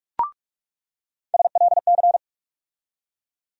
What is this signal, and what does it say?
Morse code, heard as a keyed tone, about 45 wpm, 700 Hz. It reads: SLC
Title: USA Call Sign Postfixes - 3 Letters 45wpm